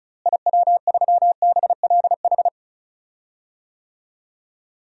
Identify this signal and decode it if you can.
Morse code, heard as a keyed tone, about 36 wpm, 700 Hz. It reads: IW3BLH
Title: DXCC-100 Call Signs - 6 Characters - Repeated 3x - 35wpm